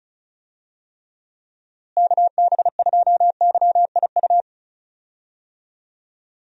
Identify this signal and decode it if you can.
Morse code, heard as a keyed tone, about 36 wpm, 700 Hz. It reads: KB2YIU
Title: USA Call Signs - 6 Characters - Repeated 3x - 35wpm